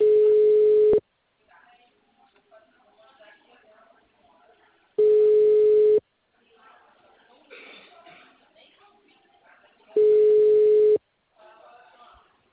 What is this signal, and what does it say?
Ringback tone (Europe, 425 Hz), with speech.
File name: out- -2010